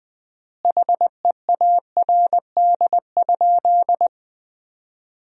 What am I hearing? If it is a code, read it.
Morse code, heard as a keyed tone, about 20 wpm, 700 Hz. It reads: HEARD?
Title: 4x Word-Spacing - Sentences from Top 400 Words - 20wpm